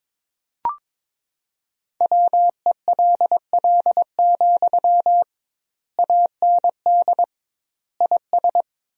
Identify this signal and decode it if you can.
Morse code, heard as a keyed tone, about 22 wpm, 700 Hz. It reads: WELL, AND IS
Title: 2x Word-Spacing - Sentences from Top 200 Words - 22wpm